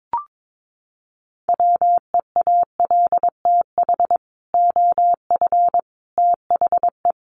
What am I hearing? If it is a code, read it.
Morse code, heard as a keyed tone, about 22 wpm, 700 Hz. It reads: WEALTH OF THE
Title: Sentences from Top 1200 Words 22wpm